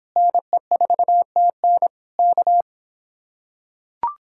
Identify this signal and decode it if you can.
Morse code, heard as a keyed tone, about 26 wpm, 700 Hz. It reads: NE4TN K